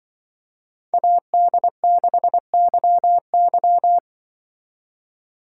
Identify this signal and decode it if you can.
Morse code, heard as a keyed tone, about 24 wpm, 700 Hz. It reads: AD6YY
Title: Speed Racing - USA 2x2 Call Signs 20wpm